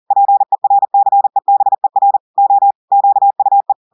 Morse code, heard as a keyed tone, about 40 wpm, 800 Hz. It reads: PERCEBER O QUE
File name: audios / portugais / portugais texte en clair 2 vitesse 40 mots minutes